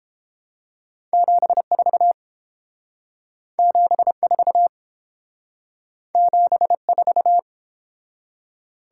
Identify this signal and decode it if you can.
Morse code, heard as a keyed tone, about 29 wpm, 700 Hz. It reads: 74 74 74